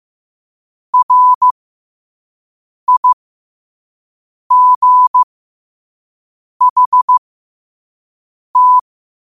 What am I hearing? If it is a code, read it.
Morse code, heard as a keyed tone, about 15 wpm, spaced as for 5 wpm, 1000 Hz. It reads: RIGHT
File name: qotdfiles / 5wpm